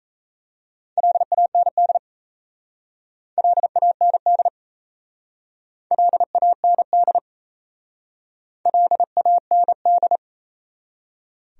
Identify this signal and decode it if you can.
Morse code, heard as a keyed tone, about 36 wpm, 700 Hz. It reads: LAND LAND LAND LAND